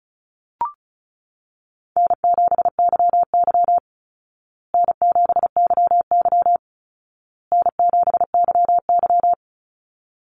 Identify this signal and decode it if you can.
Morse code, heard as a keyed tone, about 34 wpm, 700 Hz. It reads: N7YY N7YY N7YY
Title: Call Signs Repeated 2x - 35wpm